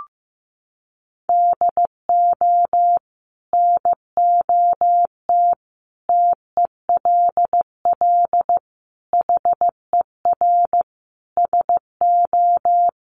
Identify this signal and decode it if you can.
Morse code, heard as a keyed tone, about 15 wpm, 700 Hz. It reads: DO NOT TELL HER SO
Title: Sentences from Top 1500 Words 15wpm